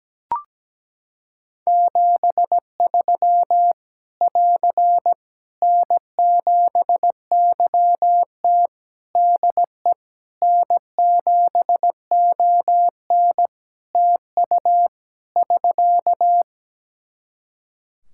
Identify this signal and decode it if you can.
Morse code, heard as a keyed tone, about 17 wpm, 700 Hz. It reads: 73 <AR> N7YT DE N7ON TU <SK>
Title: Practice QSO - Exchange 4 - Ending - 17wpm